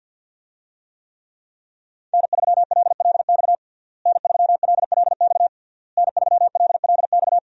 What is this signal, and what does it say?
Morse code, heard as a keyed tone, about 46 wpm, 690 Hz. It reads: N3LLX N3LLX N3LLX
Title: USA Call Signs - 5 Characters - Repeated 3x - 50wpm